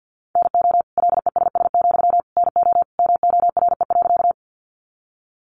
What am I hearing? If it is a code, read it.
Morse code, heard as a keyed tone, about 48 wpm, 700 Hz. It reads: NO LESS, NO MORE.
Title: Sentences from Top 400 Words 50wpm